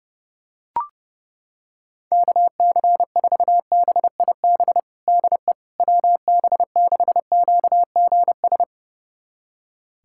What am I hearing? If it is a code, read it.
Morse code, heard as a keyed tone, about 32 wpm, 700 Hz. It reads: KC4BIB DE WB6QGS
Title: USA Callsign DE Callsign - 6 Characters - 30wpm